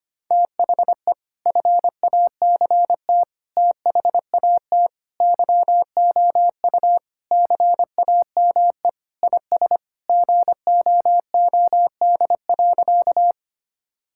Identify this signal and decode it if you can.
Morse code, heard as a keyed tone, about 25 wpm, 700 Hz. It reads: THE FACT THAT YOU CAME IS GOOD.
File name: practice / 25wpm / Sentences with 3-Word Phrases - 25wpm